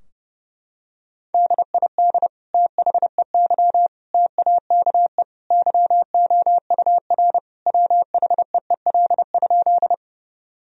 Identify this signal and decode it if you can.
Morse code, heard as a keyed tone, about 30 wpm, 700 Hz. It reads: DID THEY TAKE YOUR WHEEL?
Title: Sentences from Top 1000 Words - Review Part 3 - 30wpm